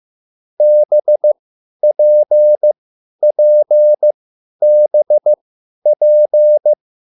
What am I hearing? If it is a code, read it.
Morse code, heard as a keyed tone, about 15 wpm, spaced as for 10 wpm, 600 Hz. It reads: BPPBP
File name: QSL / CW lesson